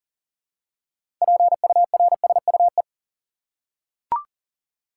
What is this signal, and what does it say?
Morse code, heard as a keyed tone, about 40 wpm, 700 Hz. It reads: PURSUE
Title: Six Letter Words - 40wpm